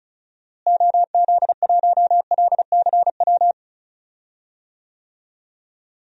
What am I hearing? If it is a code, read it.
Morse code, heard as a keyed tone, about 34 wpm, 700 Hz. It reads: OZ1LCW
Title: DXCC-100 Call Signs - 6 Characters - Repeated 2x - 35wpm